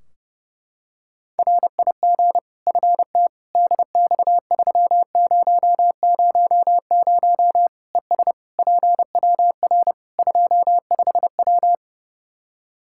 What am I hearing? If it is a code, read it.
Morse code, heard as a keyed tone, about 30 wpm, 700 Hz. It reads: RIG FT DX3000 ES PWR 25W
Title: K1USN SST QSO Practice 30wpm